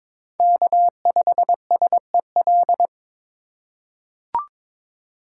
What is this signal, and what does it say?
Morse code, heard as a keyed tone, about 22 wpm, 690 Hz. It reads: K5SEL
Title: USA Call Signs - 5 Characters - 2x Character Spacing - 22wpm